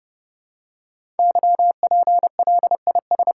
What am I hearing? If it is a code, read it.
Morse code, heard as a keyed tone, about 30 wpm, 700 Hz. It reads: YPLIS
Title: Code Groups - 5 Letters - 30wpm